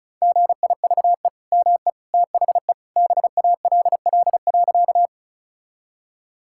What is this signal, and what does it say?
Morse code, heard as a keyed tone, about 36 wpm, 700 Hz. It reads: GIVE ME THE BALL.
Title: Sentences from Top 1000 Words - Review Part 6 - 35wpm